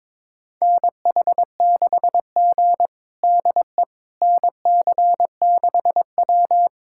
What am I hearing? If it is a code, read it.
Morse code, heard as a keyed tone, about 22 wpm, 700 Hz. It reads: NH6G DE NC6W
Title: USA Callsign DE Callsign - 4 Characters - 22wpm